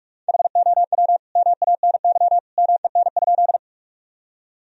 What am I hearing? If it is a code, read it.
Morse code, heard as a keyed tone, about 46 wpm, 700 Hz. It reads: HOW MANY MEN?
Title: Sentences from Top 1000 Words - Review Part 1 - 45wpm